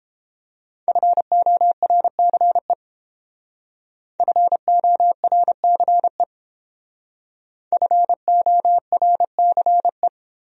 Morse code, heard as a keyed tone, about 28 wpm, 700 Hz. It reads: FORCE FORCE FORCE